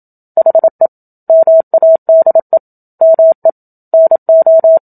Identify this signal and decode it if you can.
Morse code, heard as a keyed tone, about 27 wpm, 650 Hz. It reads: HE MADE ME NO
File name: Book Two / 01-Chapter 1 - 27 WPM